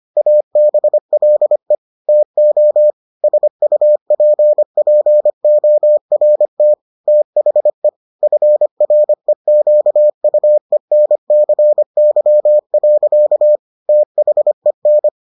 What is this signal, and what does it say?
Morse code, heard as a keyed tone, about 25 wpm, 600 Hz. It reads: ABLE TO SUPPORT THE FREQUENCY. THEN